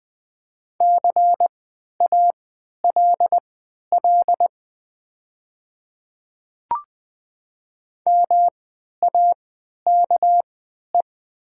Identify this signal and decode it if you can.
Morse code, heard as a keyed tone, about 20 wpm, spaced as for 11 wpm, 700 Hz. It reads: CALL MAKE
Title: 3x Character Spacing - Top 300 Words - 20wpm